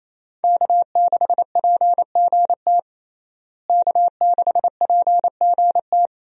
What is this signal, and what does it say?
Morse code, heard as a keyed tone, about 28 wpm, 700 Hz. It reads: K6PGT K6PGT
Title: USA Call Signs - 5 Characters - Repeated 3x - 28wpm